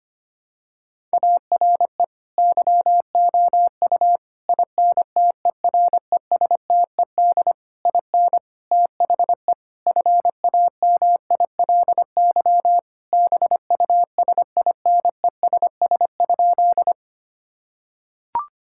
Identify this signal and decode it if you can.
Morse code, heard as a keyed tone, about 25 wpm, 700 Hz. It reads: ARE YOU INTERESTED IN THE FAMILY BUSINESS?
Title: Sentences from Top 1000 Words - Review Part 6 - 25wpm